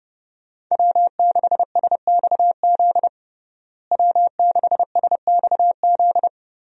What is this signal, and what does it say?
Morse code, heard as a keyed tone, about 30 wpm, 700 Hz. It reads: W6SXZ W6SXZ
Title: USA Call Signs - 5 Characters - Repeated 3x - 30wpm